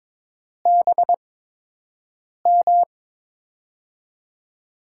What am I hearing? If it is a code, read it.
Morse code, heard as a keyed tone, about 22 wpm, spaced as for 6 wpm, 700 Hz. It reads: BM
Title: USA Call Signs - 5 Characters - 8x Character Spacing - 22wpm